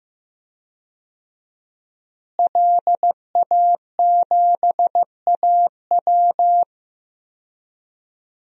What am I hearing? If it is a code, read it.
Morse code, heard as a keyed tone, about 16 wpm, 700 Hz. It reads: LA7AW
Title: DXCC-100 Call Signs - 5 Characters - Repeated 3x - 15wpm